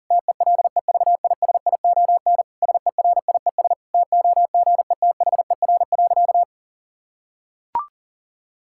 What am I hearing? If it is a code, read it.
Morse code, heard as a keyed tone, about 40 wpm, 700 Hz. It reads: TELEVISION SERIES TOGETHER.
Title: Sentences from Top 1000 Words - Review Part 5 - 40wpm